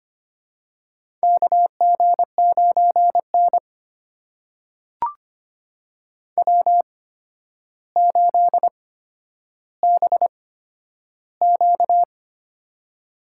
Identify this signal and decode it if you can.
Morse code, heard as a keyed tone, about 25 wpm, spaced as for 11 wpm, 700 Hz. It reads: KG9N W8BQ
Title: USA Call Signs - 4 Characters - 8x Character Spacing - 25wpm